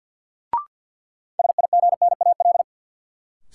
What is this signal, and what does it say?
Morse code, heard as a keyed tone, about 53 wpm, 700 Hz. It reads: SIGNAL
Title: Sentences from Top 1400 Words - New Word Drill - 50wpm